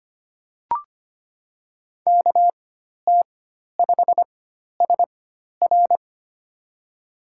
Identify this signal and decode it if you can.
Morse code, heard as a keyed tone, about 25 wpm, spaced as for 12 wpm, 700 Hz. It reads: KT5SR
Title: USA Call Signs - 5 Characters - 4x Character Spacing - 25wpm